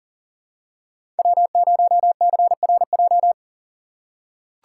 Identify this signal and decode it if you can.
Morse code, heard as a keyed tone, about 40 wpm, 700 Hz. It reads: W0CRJ
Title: USA Call Signs - 4x Character Spacing - 40wpm